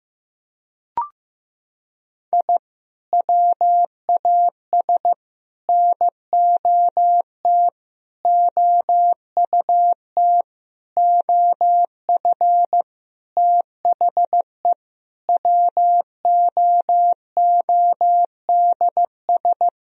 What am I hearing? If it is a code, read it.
Morse code, heard as a keyed tone, about 15 wpm, 700 Hz. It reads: I WAS NOT OUT OF THE WOODS
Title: Sentences from Top 1000 Words - Review Part 2 - 15wpm